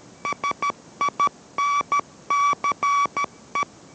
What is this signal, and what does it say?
Morse code, heard as a keyed tone, about 13 wpm, 1140 Hz. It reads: SINCE